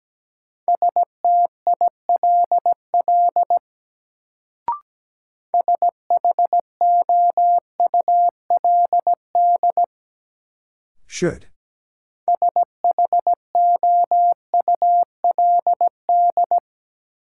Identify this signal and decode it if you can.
Morse code, heard as a keyed tone, about 17 wpm, 700 Hz. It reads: STILL SHOULD SHOULD
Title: Sentences from Top 300 Words - New Word Drill - 17wpm